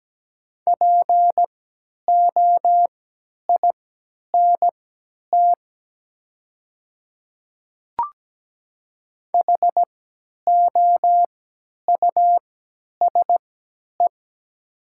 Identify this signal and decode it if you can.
Morse code, heard as a keyed tone, about 17 wpm, spaced as for 10 wpm, 700 Hz. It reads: POINT HOUSE
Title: 3x Character Spacing - Top 200 Words - 17wpm